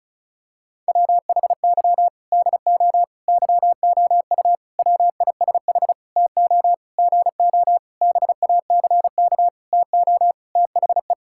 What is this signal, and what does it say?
Morse code, heard as a keyed tone, about 36 wpm, 700 Hz. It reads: WHY DO YOU WISH TO GO BACK TO THE
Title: Sentences from Top 1600 Words 35wpm